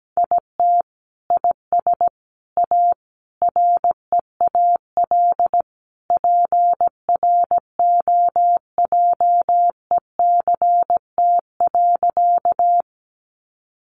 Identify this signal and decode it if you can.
Morse code, heard as a keyed tone, about 17 wpm, 700 Hz. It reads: IT IS A REAL PROJECT.
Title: Sentences from Top 1400 Words 17wpm